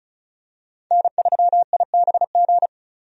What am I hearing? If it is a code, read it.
Morse code, heard as a keyed tone, about 34 wpm, 700 Hz. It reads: N3IBG
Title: USA Call Signs - 5 Characters - 35wpm